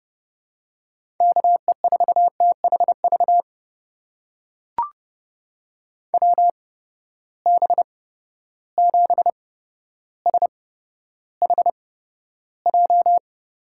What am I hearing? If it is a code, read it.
Morse code, heard as a keyed tone, about 32 wpm, spaced as for 13 wpm, 700 Hz. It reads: KE4THV WB7SHJ
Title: USA Call Signs - 6 Characters - 8x Character Spacing - 30wpm